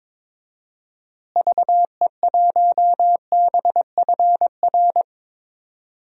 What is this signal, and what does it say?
Morse code, heard as a keyed tone, about 22 wpm, 700 Hz. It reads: VE1BFR
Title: DXCC-100 Call Signs - 6 Characters - Repeated 2x - 22wpm